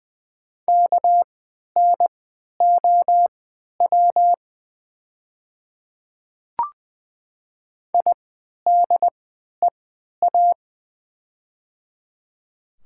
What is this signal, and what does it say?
Morse code, heard as a keyed tone, about 20 wpm, spaced as for 11 wpm, 700 Hz. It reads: KNOW IDEA
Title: 3x Character Spacing - Top 300 Words - 20wpm